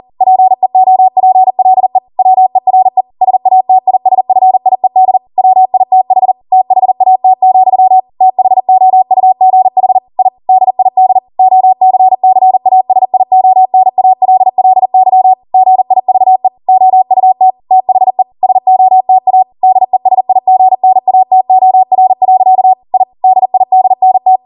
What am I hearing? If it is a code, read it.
Morse code, heard as a keyed tone, about 40 wpm, 750 Hz. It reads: PEOPLE WERE SATISFIED WITH THAT, THOUGH I DID OCCASIONALLY GIVE OUT THE SOTA DESIGNATOR. I DIDNT